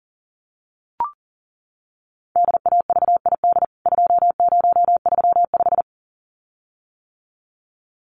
Morse code, heard as a keyed tone, about 40 wpm, 700 Hz. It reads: DAVID 2035
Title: CWT Exchange 40wpm